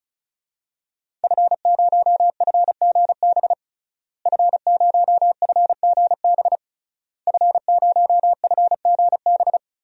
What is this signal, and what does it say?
Morse code, heard as a keyed tone, about 35 wpm, 700 Hz. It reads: F0FGB F0FGB F0FGB
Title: DXCC-100 Call Signs - 5 Characters - Repeated 3x - 35wpm